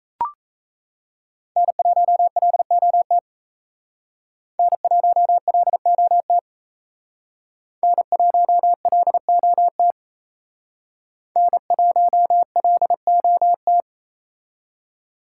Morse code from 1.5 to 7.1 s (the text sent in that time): N1LOT N1LOT